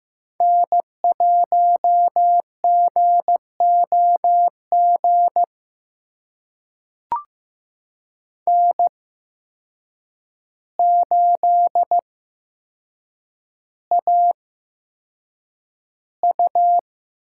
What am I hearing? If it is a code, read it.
Morse code, heard as a keyed tone, about 15 wpm, spaced as for 7 wpm, 700 Hz. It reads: N1GOG N8AU
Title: USA Call Signs - 5 Characters - 8x Character Spacing - 15wpm